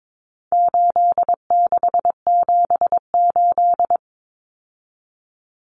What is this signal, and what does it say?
Morse code, heard as a keyed tone, about 22 wpm, 700 Hz. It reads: 8678
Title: Numbers - 4-Digit 22wpm